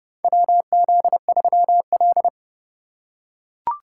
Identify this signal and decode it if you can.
Morse code, heard as a keyed tone, about 30 wpm, 700 Hz. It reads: WZ3L
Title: USA Call Signs - 4 Characters - 2x Character Spacing - 30wpm